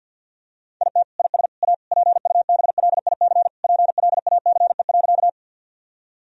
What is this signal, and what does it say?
Morse code, heard as a keyed tone, about 50 wpm, 700 Hz. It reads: IT IS A PUBLIC PLACE.
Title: Sentences from Top 1000 Words - Review Part 3 - 50wpm